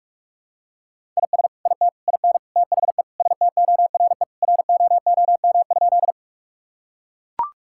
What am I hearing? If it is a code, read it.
Morse code, heard as a keyed tone, about 41 wpm, 700 Hz. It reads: IS IT IN THE STORE ROOM?
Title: Sentences from Top 1000 Words - Review Part 5 - 45wpm